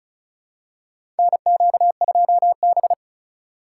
Morse code, heard as a keyed tone, about 36 wpm, 700 Hz. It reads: NQ2B
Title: USA Call Signs - 4 Characters - 3x Character Spacing - 35wpm